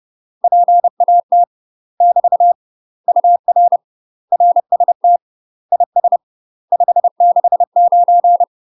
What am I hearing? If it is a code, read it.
Morse code, heard as a keyed tone, about 30 wpm, 700 Hz. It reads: PAT <BT> UR RST IS 569
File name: UK MASTER QSOs TRAINING / 30WPM / UK MASTER QSOs 0050